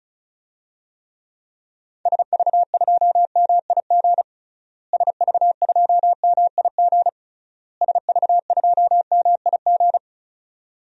Morse code, heard as a keyed tone, about 36 wpm, 700 Hz. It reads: SV2MIG SV2MIG SV2MIG
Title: DXCC-100 Call Signs - 6 Characters - Repeated 3x - 35wpm